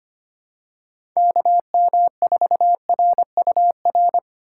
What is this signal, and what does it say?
Morse code, heard as a keyed tone, about 25 wpm, 700 Hz. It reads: KM4RUR
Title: USA Call Signs - 6 Characters - 2x Character Spacing - 25wpm